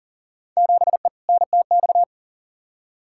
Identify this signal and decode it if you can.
Morse code, heard as a keyed tone, about 38 wpm, 700 Hz. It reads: 7E NTX
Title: Field Day Exchanges Encore - 40wpm